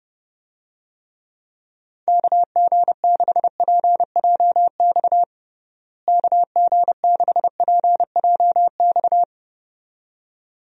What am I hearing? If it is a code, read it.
Morse code, heard as a keyed tone, about 32 wpm, 700 Hz. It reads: KG6PJX KG6PJX
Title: USA Call Signs - 6 Characters - Repeated 2x - 30wpm